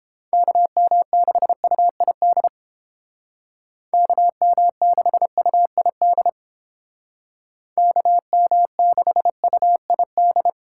Morse code, heard as a keyed tone, about 29 wpm, 700 Hz. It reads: KM6UID KM6UID KM6UID